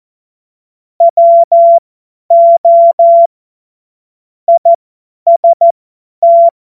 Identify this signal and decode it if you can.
Morse code, heard as a keyed tone, about 14 wpm, spaced as for 10 wpm, 680 Hz. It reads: WO IST
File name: Morsekurs / Witze / Witz 16. August 2022 50 F70